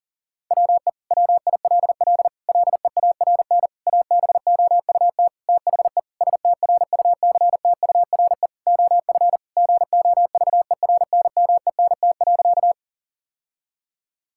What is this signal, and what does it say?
Morse code, heard as a keyed tone, about 37 wpm, 700 Hz. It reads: WE WILL LEARN ABOUT THE STRUCTURE OF GOVERNMENT.
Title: Sentences from Top 1000 Words - Review Part 6 - 40wpm